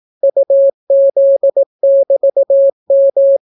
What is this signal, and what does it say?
Morse code, heard as a keyed tone, about 18 wpm, 550 Hz. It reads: UZ<BT>M